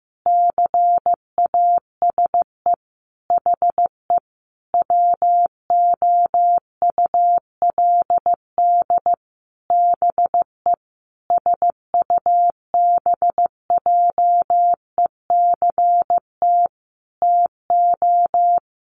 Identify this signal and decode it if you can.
Morse code, heard as a keyed tone, about 15 wpm, 700 Hz. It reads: CASE HE WOULD BE SUBJECT TO
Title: Sentences from Top 1300 Words 15wpm